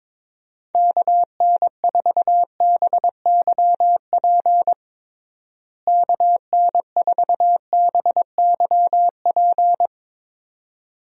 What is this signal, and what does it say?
Morse code, heard as a keyed tone, about 22 wpm, 700 Hz. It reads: KN4BYP KN4BYP
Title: USA Call Signs - 6 Characters - Repeated 2x - 22wpm